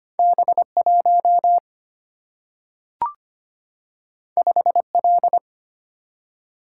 Morse code, heard as a keyed tone, about 25 wpm, 700 Hz. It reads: B1 5L
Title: Single-Letter Single-Number 25wpm